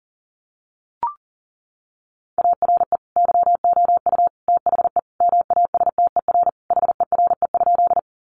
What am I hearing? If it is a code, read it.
Morse code, heard as a keyed tone, about 37 wpm, 700 Hz. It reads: ARE YOU THE MASTER HERE?
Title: Sentences from Top 1300 Words 40wpm